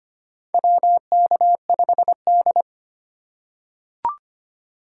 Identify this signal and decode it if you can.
Morse code, heard as a keyed tone, about 25 wpm, 700 Hz. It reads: WK5D
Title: USA Call Signs - 3x Character Spacing - 25wpm